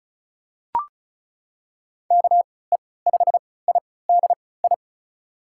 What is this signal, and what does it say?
Morse code, heard as a keyed tone, about 36 wpm, spaced as for 20 wpm, 700 Hz. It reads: KE5IDI
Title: USA Call Signs - 6 Characters - 3x Character Spacing - 35wpm